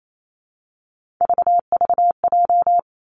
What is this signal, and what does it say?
Morse code, heard as a keyed tone, about 28 wpm, 700 Hz. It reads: VVJ
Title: Three Challenging Letter-Number Combinations 28wpm